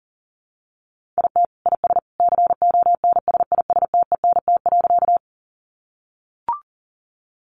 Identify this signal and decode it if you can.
Morse code, heard as a keyed tone, about 40 wpm, 700 Hz. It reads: IT IS CONSISTENT.
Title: Sentences from Top 1500 Words 40wpm